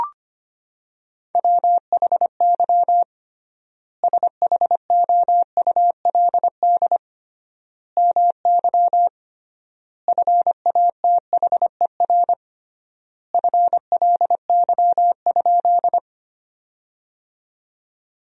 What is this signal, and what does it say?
Morse code, heard as a keyed tone, about 25 wpm, 700 Hz. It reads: WHY SHOULD MY FATHER FLY?
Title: 3x Word-Spacing - Sentences from Top 500 Words - 25wpm